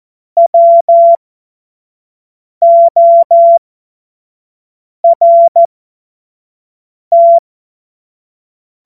Morse code, heard as a keyed tone, about 14 wpm, spaced as for 5 wpm, 680 Hz. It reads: WORT
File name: Morsekurs / Witze / Witz 08. November 2022 25 F70